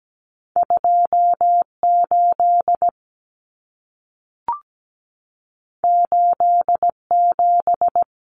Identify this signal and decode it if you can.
Morse code, heard as a keyed tone, about 17 wpm, 700 Hz. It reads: 28 87
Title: Common Cut Numbers - 2-Digit 17wpm